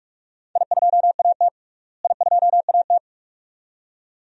Morse code, heard as a keyed tone, about 45 wpm, 700 Hz. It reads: I2AT I2AT
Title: DXCC-100 Call Signs - 4 Characters - Repeated 2x - 45wpm